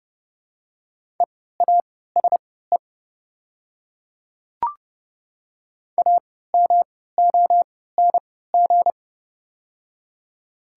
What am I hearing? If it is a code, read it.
Morse code, heard as a keyed tone, about 30 wpm, spaced as for 17 wpm, 700 Hz. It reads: EASE AMONG
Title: 3x Character Spacing - Top 500 Words - 30wpm